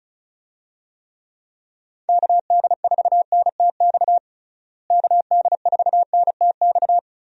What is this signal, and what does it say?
Morse code, heard as a keyed tone, about 36 wpm, 700 Hz. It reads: KD4NTX KD4NTX
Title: USA Call Signs - 6 Characters - Repeated 2x - 35wpm